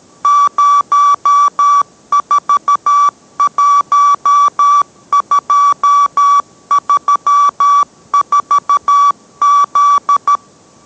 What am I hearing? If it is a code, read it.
Morse code, heard as a keyed tone, about 13 wpm, 1150 Hz. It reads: 041234Z